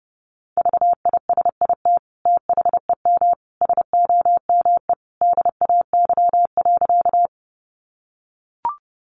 Morse code, heard as a keyed tone, about 30 wpm, 700 Hz. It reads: VISIT THEM SOME DAY.
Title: Sentences from Top 900 Words 30wpm